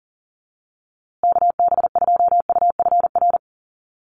Morse code, heard as a keyed tone, about 40 wpm, 700 Hz. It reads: KB2UFR
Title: USA Call Signs Repeated 3x 40wpm